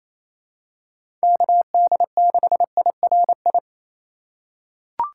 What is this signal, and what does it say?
Morse code, heard as a keyed tone, about 28 wpm, 700 Hz. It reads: KD6IRI
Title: K1USN SST QSO Practice - 28wpm